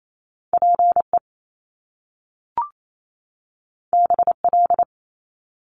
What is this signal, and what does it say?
Morse code, heard as a keyed tone, about 28 wpm, 700 Hz. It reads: PE BL
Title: Top Two Letter Combinations in Words 28wpm